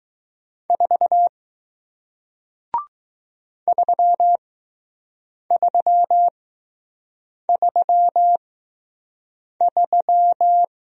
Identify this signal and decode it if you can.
Morse code, heard as a keyed tone, about 20 wpm, spaced as for 6 wpm, 700 Hz. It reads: 4 3333